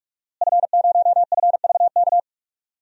Morse code, heard as a keyed tone, about 46 wpm, 700 Hz. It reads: F0FVK
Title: DXCC-100 Call Signs - 5 Characters - Repeated 3x - 45wpm